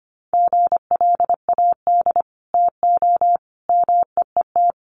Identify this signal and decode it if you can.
Morse code, heard as a keyed tone, about 25 wpm, 700 Hz. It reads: GLAD TO MEET
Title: Sentences from Top 1100 Words 25wpm